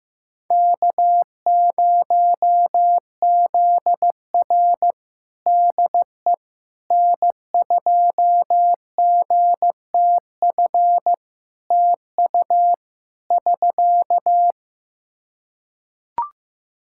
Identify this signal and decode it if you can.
Morse code, heard as a keyed tone, about 15 wpm, 700 Hz. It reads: K0ZR DE N2GTF TU <SK>
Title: Practice QSO - Exchange 4 - Ending - 15wpm